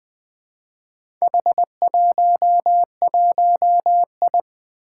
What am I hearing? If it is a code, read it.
Morse code, heard as a keyed tone, about 20 wpm, 700 Hz. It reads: H11I
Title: Four Challenging Letter-Number Combinations 20wpm